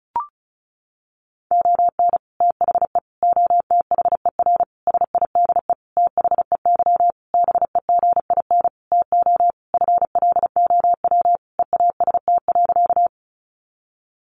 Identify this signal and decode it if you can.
Morse code, heard as a keyed tone, about 34 wpm, 700 Hz. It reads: ON THE OTHER SIDE THEY BEGIN TO FLOW EAST.
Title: Sentences from Top 1200 Words 35wpm